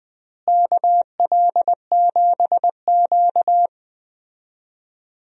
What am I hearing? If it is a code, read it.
Morse code, heard as a keyed tone, about 20 wpm, 700 Hz. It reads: KL7Q
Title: DXCC-100 Call Signs - 4 Characters - Repeated 3x - 20wpm